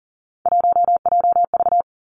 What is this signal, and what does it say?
Morse code, heard as a keyed tone, about 40 wpm, 700 Hz. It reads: 1JV
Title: Three Challenging Letter-Number Combinations 40wpm